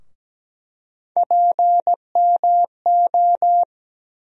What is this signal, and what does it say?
Morse code, heard as a keyed tone, about 17 wpm, 700 Hz. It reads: PMO
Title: USA Call Sign Postfixes - 3 Letters 17wpm